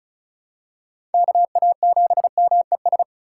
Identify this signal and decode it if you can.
Morse code, heard as a keyed tone, about 34 wpm, 690 Hz. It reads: KA7MES
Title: USA Call Signs - 6 Characters - 6x Character Spacing - 35wpm